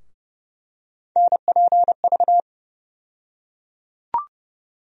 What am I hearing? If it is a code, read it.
Morse code, heard as a keyed tone, about 29 wpm, 700 Hz. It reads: NPV
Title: USA Call Sign Postfixes - 3 Letters 30wpm